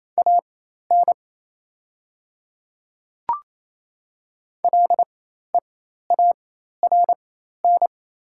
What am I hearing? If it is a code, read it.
Morse code, heard as a keyed tone, about 28 wpm, spaced as for 13 wpm, 700 Hz. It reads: AN LEARN
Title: 4x Character Spacing - Top 300 Words - 28wpm